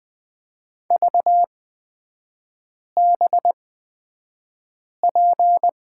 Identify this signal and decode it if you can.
Morse code, heard as a keyed tone, about 20 wpm, spaced as for 5 wpm, 700 Hz. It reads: VBP
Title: 224 Lesson 36e - BT - Three Characters 20wpm